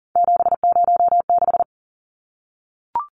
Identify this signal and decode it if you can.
Morse code, heard as a keyed tone, about 40 wpm, 700 Hz. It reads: 706
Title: Numbers - 3-Digit 40wpm